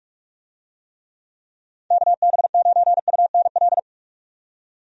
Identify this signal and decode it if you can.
Morse code, heard as a keyed tone, about 41 wpm, 700 Hz. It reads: KB9UNL
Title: USA Call Signs - 6 Characters - 45wpm